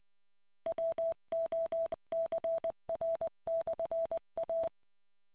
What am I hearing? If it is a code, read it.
Morse code, heard as a keyed tone, about 21 wpm, 660 Hz. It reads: WOCR/A